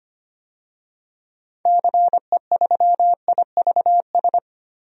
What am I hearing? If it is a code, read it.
Morse code, heard as a keyed tone, about 25 wpm, 700 Hz. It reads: CE3IVS
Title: DXCC-100 Call Signs - 6 Characters - Repeated 2x - 25wpm